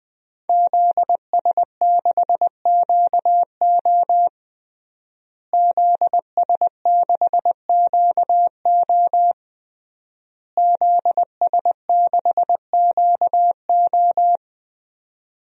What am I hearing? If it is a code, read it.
Morse code, heard as a keyed tone, about 20 wpm, 700 Hz. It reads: ZS6QO ZS6QO ZS6QO
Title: DXCC-100 Call Signs - 5 Characters - Repeated 3x - 20wpm